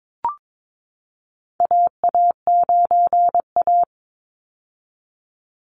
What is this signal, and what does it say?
Morse code, heard as a keyed tone, about 22 wpm, 700 Hz. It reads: AA9A
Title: Call Signs - 22wpm